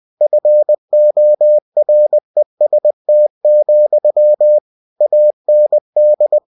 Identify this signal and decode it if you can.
Morse code, heard as a keyed tone, about 20 wpm, 600 Hz. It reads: FOREST, AND